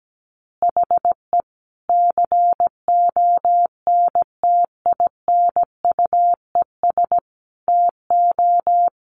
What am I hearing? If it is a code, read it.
Morse code, heard as a keyed tone, about 17 wpm, 700 Hz. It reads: HE CONTINUES TO
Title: Easy Phrases 17wpm